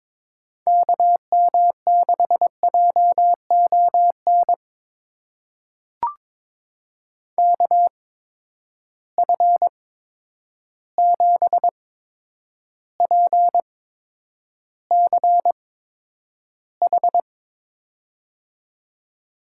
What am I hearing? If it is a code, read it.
Morse code, heard as a keyed tone, about 22 wpm, spaced as for 9 wpm, 700 Hz. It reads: KM6JON KF7PCH
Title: USA Call Signs - 6 Characters - 8x Character Spacing - 22wpm